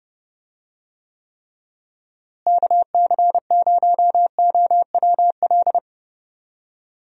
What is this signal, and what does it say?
Morse code, heard as a keyed tone, about 30 wpm, 700 Hz. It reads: KC0OWL
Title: USA Call Signs - 6 Characters - Repeated 3x - 30wpm